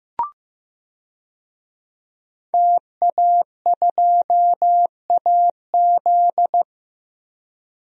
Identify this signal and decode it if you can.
Morse code, heard as a keyed tone, about 15 wpm, 700 Hz. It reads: TA2AZ
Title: DXCC-100 Call Signs - 5 Characters - 15wpm